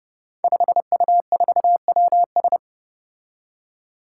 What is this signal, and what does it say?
Morse code, heard as a keyed tone, about 28 wpm, 700 Hz. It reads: 5U4WS
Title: Five Challenging Letter-Number Combinations 30wpm